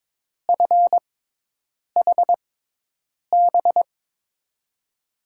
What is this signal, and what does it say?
Morse code, heard as a keyed tone, about 22 wpm, spaced as for 8 wpm, 700 Hz. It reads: FHB
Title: USA Call Signs - 5 Characters - 6x Character Spacing - 22wpm